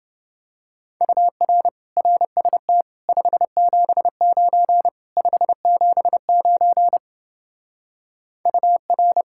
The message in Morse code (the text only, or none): UR RST 579 579 UR